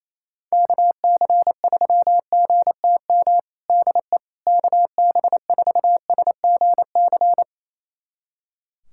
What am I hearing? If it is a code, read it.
Morse code, heard as a keyed tone, about 28 wpm, 700 Hz. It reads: KC3GTM DE KB4SGC
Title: USA Callsign DE Callsign - 6 Characters - 28wpm